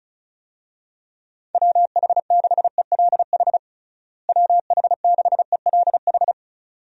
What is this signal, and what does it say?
Morse code, heard as a keyed tone, about 36 wpm, 700 Hz. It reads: WH6ELH WH6ELH
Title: USA Call Signs - 6 Characters - Repeated 2x - 35wpm